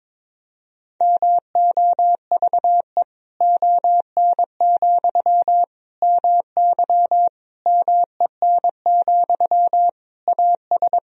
Morse code, heard as a keyed tone, about 22 wpm, 700 Hz. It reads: MOVE ON, MY MEN, AS
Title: Sentences from Top 500 Words and Abbreviations 22wpm